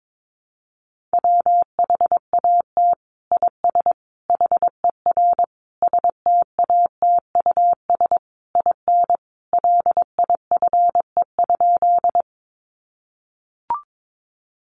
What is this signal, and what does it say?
Morse code, heard as a keyed tone, about 22 wpm, 700 Hz. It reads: WHAT IS HER STATUS IN LIFE?
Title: Sentences from Top 1400 Words 22wpm